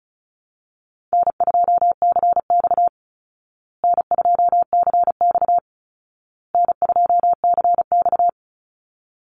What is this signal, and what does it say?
Morse code, heard as a keyed tone, about 34 wpm, 700 Hz. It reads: N2CX N2CX N2CX
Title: Call Signs Repeated 2x - 35wpm